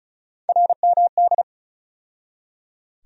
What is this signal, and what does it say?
Morse code, heard as a keyed tone, about 36 wpm, 700 Hz. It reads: RMD
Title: USA Call Sign Postfixes - 3 Letters 35wpm